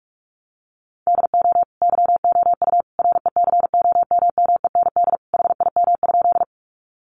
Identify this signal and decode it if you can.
Morse code, heard as a keyed tone, about 44 wpm, 700 Hz. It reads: DO YOU RECOMMEND HIM?
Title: Sentences from Top 1200 Words 45wpm